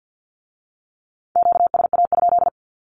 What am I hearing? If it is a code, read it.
Morse code, heard as a keyed tone, about 50 wpm, 700 Hz. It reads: QSA?
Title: Q-Codes with Spoken Meaning - 50wpm